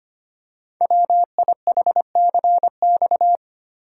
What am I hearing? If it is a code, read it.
Morse code, heard as a keyed tone, about 25 wpm, 700 Hz. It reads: WIHCX
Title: Code Groups - 5 Letters - 25wpm